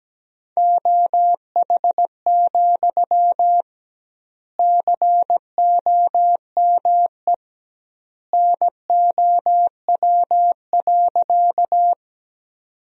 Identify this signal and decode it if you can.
Morse code, heard as a keyed tone, about 17 wpm, 700 Hz. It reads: OH, COME NOW.
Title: 2x Word-Spacing - Sentences from Top 500 Words - 17wpm